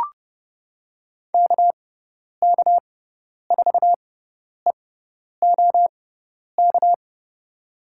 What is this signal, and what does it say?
Morse code, heard as a keyed tone, about 31 wpm, spaced as for 10 wpm, 700 Hz. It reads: KK4EOK
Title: USA Call Signs - 6x Character Spacing - 30wpm